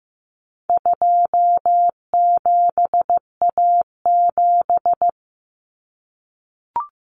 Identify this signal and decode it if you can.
Morse code, heard as a keyed tone, about 15 wpm, 700 Hz. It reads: 27A7
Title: Common Cut Numbers - 4-Digit 15wpm